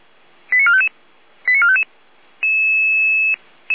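Five-tone selective call (ZVEI), heard as quick runs of five tones, tones of about 70 ms. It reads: 89455 89455
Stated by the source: ZVEI five-tone selective call received on an SDR.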